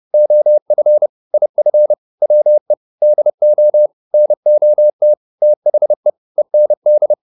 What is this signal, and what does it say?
Morse code, heard as a keyed tone, about 30 wpm, 600 Hz. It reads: OF IF WE DO NOT THE END